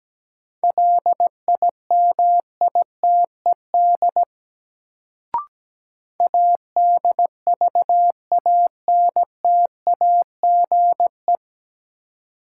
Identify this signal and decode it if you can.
Morse code, heard as a keyed tone, about 17 wpm, 700 Hz. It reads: LIMITED ADVANTAGE
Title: Sentences from Top 900 Words - New Word Drill - 17wpm